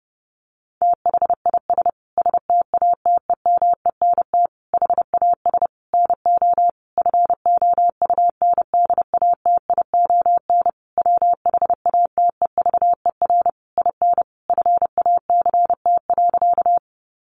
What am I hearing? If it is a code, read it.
Morse code, heard as a keyed tone, about 30 wpm, 700 Hz. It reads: THIS STATEMENT HAS NO FOUNDATION WHATEVER IN FACT.
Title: Sentences from Top 1200 Words 30wpm